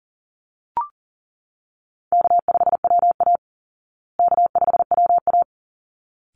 Morse code, heard as a keyed tone, about 40 wpm, 700 Hz. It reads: K5WA K5WA
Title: Easy Call Signs Repeated 2x - 40wpm